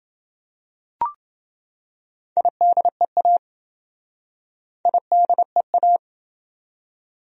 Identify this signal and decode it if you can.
Morse code, heard as a keyed tone, about 30 wpm, 700 Hz. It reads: IDEA IDEA